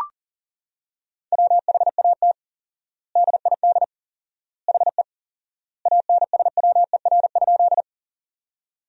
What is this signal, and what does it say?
Morse code, heard as a keyed tone, about 40 wpm, 700 Hz. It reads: WHAT DID HE ANSWER?
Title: 4x Word-Spacing - Sentences from Top 300 Words - 40wpm